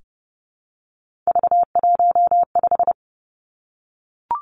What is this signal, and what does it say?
Morse code, heard as a keyed tone, about 30 wpm, 700 Hz. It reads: V15
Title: Three Challenging Letter-Number Combinations 30wpm